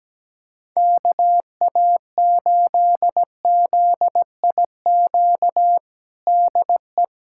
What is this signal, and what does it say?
Morse code, heard as a keyed tone, about 17 wpm, 700 Hz. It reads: KA8ZIQ DE
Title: USA Callsign DE Callsign 17wpm